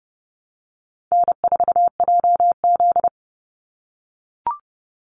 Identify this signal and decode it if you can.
Morse code, heard as a keyed tone, about 30 wpm, 700 Hz. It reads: N4JZ
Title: USA Call Signs 30wpm